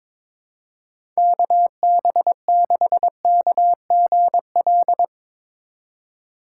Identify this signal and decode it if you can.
Morse code, heard as a keyed tone, about 21 wpm, 700 Hz. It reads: KB6KGL
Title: USA Call Signs - 3x Character Spacing - 22wpm